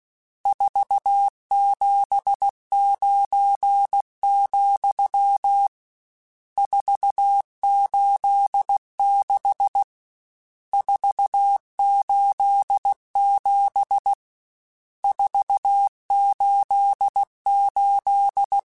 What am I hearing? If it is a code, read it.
Morse code, heard as a keyed tone, about 16 wpm, 800 Hz. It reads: 479, 486 487 488
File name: RSTs / RST-mp3s / Various RST @15